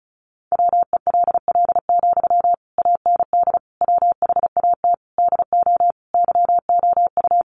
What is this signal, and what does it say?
Morse code, heard as a keyed tone, about 34 wpm, 700 Hz. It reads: WELL, AND WHAT DO YOU
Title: Sentences from Top 200 Words 35wpm